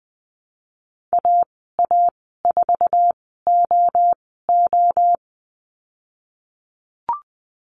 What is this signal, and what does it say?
Morse code, heard as a keyed tone, about 20 wpm, spaced as for 15 wpm, 700 Hz. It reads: AA4OO
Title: 088 Lesson 15g - Number 5 - Call Signs 20wpm